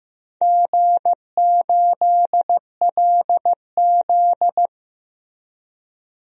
Morse code, heard as a keyed tone, about 15 wpm, 700 Hz. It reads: G8LZ
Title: DXCC-100 Call Signs - 4 Characters - 15wpm